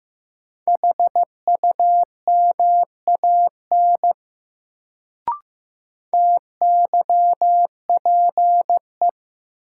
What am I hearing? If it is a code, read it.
Morse code, heard as a keyed tone, about 15 wpm, 700 Hz. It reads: HUMAN TYPE
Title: Sentences from Top 600 Words - New Word Drill - 15wpm